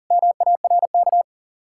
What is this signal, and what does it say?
Morse code, heard as a keyed tone, about 50 wpm, 700 Hz. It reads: MARK
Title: Speed-Racing Common QSO Phrases Encore - 40wpm